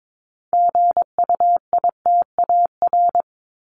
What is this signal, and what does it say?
Morse code, heard as a keyed tone, about 22 wpm, 700 Hz. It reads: GUITAR
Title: Top 1501-1750 Words 22wpm